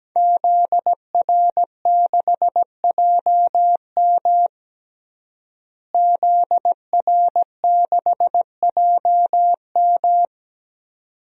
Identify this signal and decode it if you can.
Morse code, heard as a keyed tone, about 17 wpm, 700 Hz. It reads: ZR6JM ZR6JM
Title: DXCC-100 Call Signs - 5 Characters - Repeated 2x - 17wpm